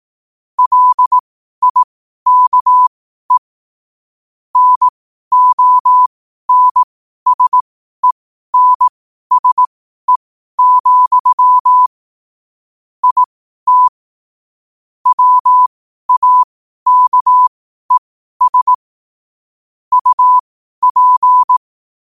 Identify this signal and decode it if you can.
Morse code, heard as a keyed tone, about 18 wpm, spaced as for 12 wpm, 1000 Hz. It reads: LIKE NONSENSE, IT WAKES UP